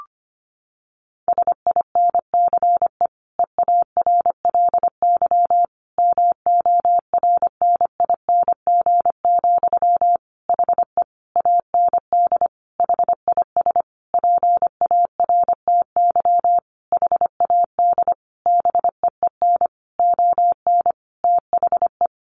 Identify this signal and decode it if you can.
Morse code, heard as a keyed tone, about 25 wpm, 700 Hz. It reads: SINCE EARLY MORNING, HE AND HIS PARTY HAD BEEN ON THE
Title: Sentences from Top 1400 Words 25wpm